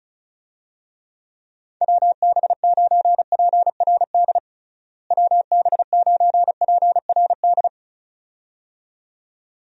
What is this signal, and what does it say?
Morse code, heard as a keyed tone, about 34 wpm, 700 Hz. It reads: WB9PRD WB9PRD
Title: USA Call Signs - 6 Characters - Repeated 2x - 35wpm